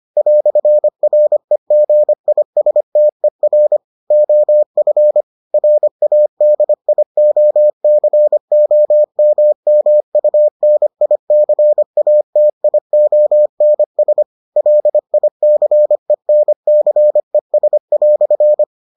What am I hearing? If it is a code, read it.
Morse code, heard as a keyed tone, about 25 wpm, 600 Hz. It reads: "REGISTER OF RADIOCOMMUNICATIONS LICENCES"